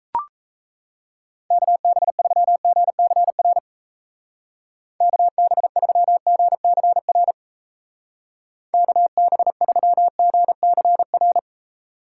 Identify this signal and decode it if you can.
Morse code, heard as a keyed tone, about 34 wpm, 700 Hz. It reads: KB3GCR KB3GCR KB3GCR
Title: Speed Racing - USA 2x3 Call Signs 28wpm